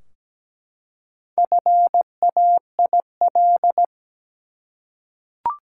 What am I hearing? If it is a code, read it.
Morse code, heard as a keyed tone, about 17 wpm, 700 Hz. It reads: FAIL
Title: Four Letter Words - 17wpm